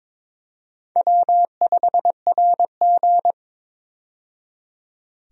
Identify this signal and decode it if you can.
Morse code, heard as a keyed tone, about 22 wpm, 700 Hz. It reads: W5RG
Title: USA Call Signs - 4 Characters - 22wpm